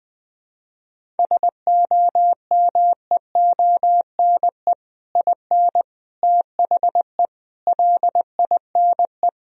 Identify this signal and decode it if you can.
Morse code, heard as a keyed tone, about 20 wpm, 700 Hz. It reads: SOMEONE IN THE LINE
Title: Sentences from Top 1000 Words - Review Part 1 - 20wpm